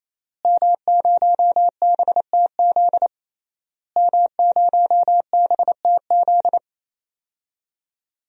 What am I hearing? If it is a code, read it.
Morse code, heard as a keyed tone, about 28 wpm, 700 Hz. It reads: M0BTZ M0BTZ
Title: DXCC-100 Call Signs - 5 Characters - Repeated 3x - 28wpm